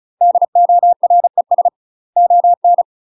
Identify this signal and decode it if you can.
Morse code, heard as a keyed tone, about 34 wpm, 700 Hz. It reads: DORES ON
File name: UK MASTER QSOs TRAINING / 35WPM / UK MASTER QSOs 0015